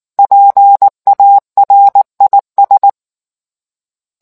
Morse code, heard as a keyed tone, about 19 wpm, 800 Hz. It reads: PARIS